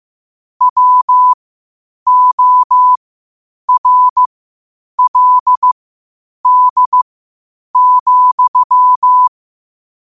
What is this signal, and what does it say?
Morse code, heard as a keyed tone, about 15 wpm, spaced as for 8 wpm, 1000 Hz. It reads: WORLD,
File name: qotdfiles / 8wpm